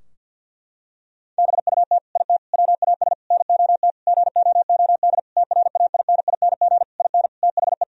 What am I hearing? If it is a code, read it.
Morse code, heard as a keyed tone, about 52 wpm, 700 Hz. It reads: BUT IT WAS NOT GOOD TRAINING IN THE
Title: Sentences from Top 1000 Words - Review Part 4 - 50wpm